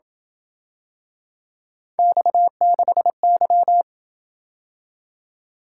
Morse code, heard as a keyed tone, about 26 wpm, 700 Hz. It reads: X6Y